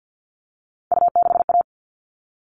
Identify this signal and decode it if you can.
Morse code, heard as a keyed tone, about 50 wpm, 700 Hz. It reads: U6A
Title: All Cut Numbers - 3-Digit 50wpm